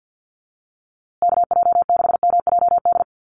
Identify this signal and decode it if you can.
Morse code, heard as a keyed tone, about 50 wpm, 700 Hz. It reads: KJ6MJD
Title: USA Call Signs Repeated 3x 50wpm